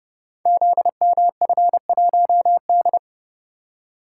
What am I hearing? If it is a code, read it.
Morse code, heard as a keyed tone, about 30 wpm, 700 Hz. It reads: ZMF1D
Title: Code Groups - 5 Letter:Numbers - 30wpm